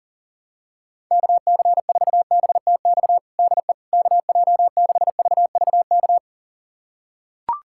Morse code, heard as a keyed tone, about 40 wpm, 700 Hz. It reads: KC4BTX DE KJ6VVK
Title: USA Callsign DE Callsign - 6 Characters - 40wpm